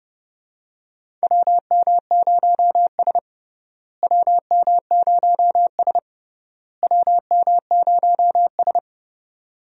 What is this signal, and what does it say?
Morse code, heard as a keyed tone, about 31 wpm, 700 Hz. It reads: WM0S WM0S WM0S
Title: USA Call Signs - 4 Characters - Repeated 3x - 30wpm